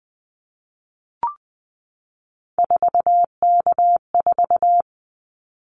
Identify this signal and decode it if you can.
Morse code, heard as a keyed tone, about 20 wpm, 700 Hz. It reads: 4K4
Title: Callsign Prefixes for Top 100 DXCC - 20wpm